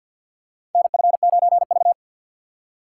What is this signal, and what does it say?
Morse code, heard as a keyed tone, about 47 wpm, 690 Hz. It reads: NF9V
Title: USA Call Signs - 4 Characters - 6x Character Spacing - 50wpm